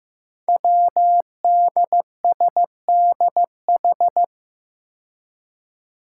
Morse code, heard as a keyed tone, about 15 wpm, 700 Hz. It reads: WDSDH
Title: Five Challenging Letter-Number Combinations 15wpm